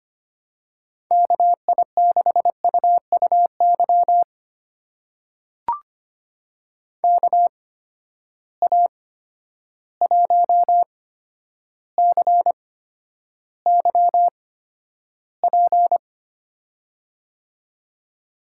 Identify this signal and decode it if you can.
Morse code, heard as a keyed tone, about 26 wpm, spaced as for 11 wpm, 700 Hz. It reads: KI6UUY KA1CYP
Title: USA Call Signs - 8x Character Spacing - 25wpm